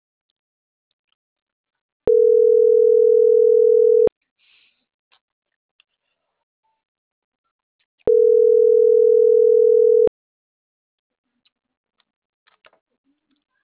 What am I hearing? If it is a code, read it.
Ringback tone (North America), 440 + 480 Hz.
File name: CENTRAL MAYAB / out atzhyry.gomez